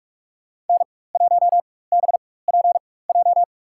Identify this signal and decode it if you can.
Morse code, heard as a keyed tone, about 46 wpm, spaced as for 21 wpm, 700 Hz. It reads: N1BPJ
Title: USA Call Signs - 5 Characters - 4x Character Spacing - 45wpm